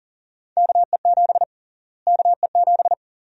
Morse code, heard as a keyed tone, about 40 wpm, 700 Hz. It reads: KE7 KE7
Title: USA Call Sign Prefixes - Repeated 2x - 40wpm